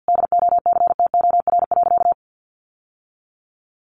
Morse code, heard as a keyed tone, about 50 wpm, 700 Hz. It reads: DOCTOR.
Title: Sentences from Top 1500 Words 50wpm